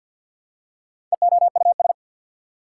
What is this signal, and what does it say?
Morse code, heard as a keyed tone, about 48 wpm, 690 Hz. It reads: EOUS
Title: Prefixes and Suffixes 50wpm